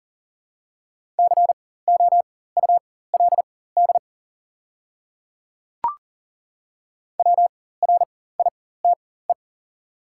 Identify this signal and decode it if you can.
Morse code, heard as a keyed tone, about 42 wpm, spaced as for 19 wpm, 690 Hz. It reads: COULD WRITE
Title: 4x Character Spacing - Top 100 Words - 40wpm